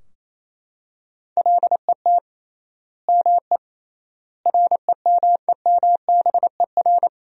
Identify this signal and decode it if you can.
Morse code, heard as a keyed tone, about 28 wpm, 700 Hz. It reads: LET ME REMEMBER
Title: 3x Word-Spacing - Sentences from Top 400 Words - 28wpm